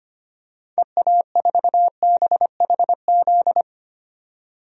Morse code, heard as a keyed tone, about 25 wpm, 690 Hz. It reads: EA4BHZ
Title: DXCC-100 Call Signs - 6 Characters - 25wpm